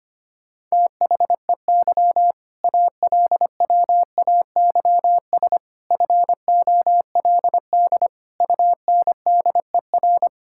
Morse code, heard as a keyed tone, about 25 wpm, 700 Hz. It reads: THEY ALWAYS FOLD UNDER
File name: practice / 25wpm / All-Age Appropriate Jokes - 25wpm